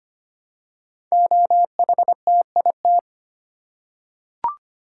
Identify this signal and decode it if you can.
Morse code, heard as a keyed tone, about 25 wpm, 700 Hz. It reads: OHTIT
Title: Code Groups - 5 Letters - 25wpm